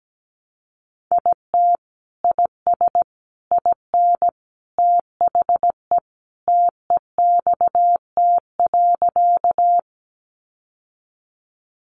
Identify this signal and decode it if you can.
Morse code, heard as a keyed tone, about 17 wpm, 700 Hz. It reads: IT IS IN THE TEXT.
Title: Sentences from Top 1200 Words 17wpm